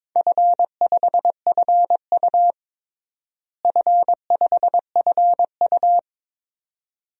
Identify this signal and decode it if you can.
Morse code, heard as a keyed tone, about 22 wpm, 700 Hz. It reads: F5FU F5FU
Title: DXCC-100 Call Signs - 4 Characters - Repeated 2x - 22wpm